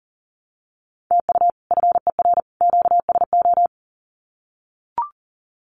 Morse code, heard as a keyed tone, about 40 wpm, 700 Hz. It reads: TU FER QSO
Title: Common QSO Phrases 40wpm